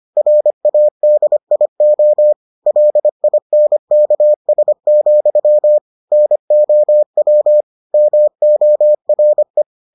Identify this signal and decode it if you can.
Morse code, heard as a keyed tone, about 25 wpm, 600 Hz. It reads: RADIO LINKS, NOW MORE